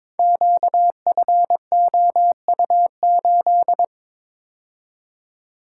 Morse code, heard as a keyed tone, about 22 wpm, 700 Hz. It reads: QFOU8
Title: Code Groups - 5 Letter:Numbers - 22wpm